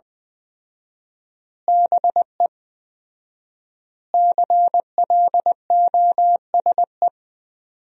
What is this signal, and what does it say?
Morse code, heard as a keyed tone, about 21 wpm, 700 Hz. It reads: BE CLOSE
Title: 4x Word-Spacing - Sentences from Top 300 Words - 20wpm